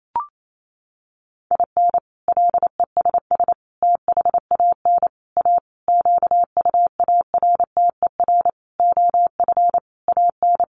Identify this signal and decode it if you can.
Morse code, heard as a keyed tone, about 28 wpm, 700 Hz. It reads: IN LESS THAN A QUARTER OF AN
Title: Sentences from Top 1300 Words 28wpm